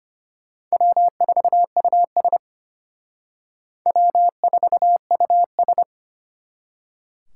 Morse code, heard as a keyed tone, about 32 wpm, 700 Hz. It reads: W4US W4US